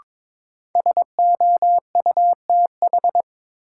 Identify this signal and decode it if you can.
Morse code, heard as a keyed tone, about 22 wpm, 700 Hz. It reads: SOUTH
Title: Sentences from Top 400 Words - New Word Drill - 22wpm